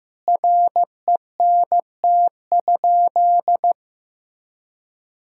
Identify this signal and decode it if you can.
Morse code, heard as a keyed tone, about 15 wpm, 700 Hz. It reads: RENT?
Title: Sentences from Top 1000 Words - Review Part 6 - 15wpm